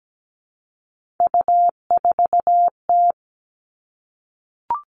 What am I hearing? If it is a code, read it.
Morse code, heard as a keyed tone, about 17 wpm, 700 Hz. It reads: U4T
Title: All Cut Numbers - 3-Digit 17wpm